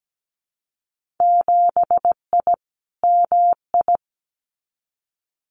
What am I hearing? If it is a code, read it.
Morse code, heard as a keyed tone, about 17 wpm, 700 Hz. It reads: 7I MI
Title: Winter Field Day Exchanges Encore 17wpm